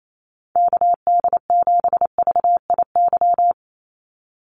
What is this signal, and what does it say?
Morse code, heard as a keyed tone, about 28 wpm, 700 Hz. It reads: KD7VIY
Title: USA Call Signs 28wpm